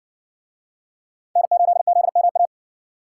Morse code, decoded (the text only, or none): N8ZMA